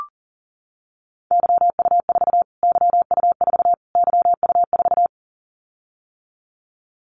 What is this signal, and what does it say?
Morse code, heard as a keyed tone, about 40 wpm, 700 Hz. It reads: YU4 YU4 YU4
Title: Callsign Prefixes for Top 25 DXCC - Repeated 3x - 40wpm